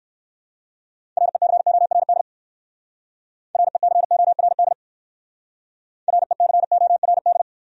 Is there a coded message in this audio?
Yes: Morse code from 1.2 to 7.4 s, heard as a keyed tone, about 58 wpm, 700 Hz. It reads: RECORD RECORD RECORD